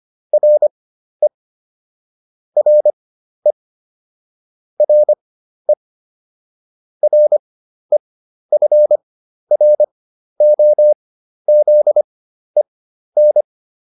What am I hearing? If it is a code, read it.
Morse code, heard as a keyed tone, about 25 wpm, spaced as for 12 wpm, 600 Hz. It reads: RE RE RE REFROZEN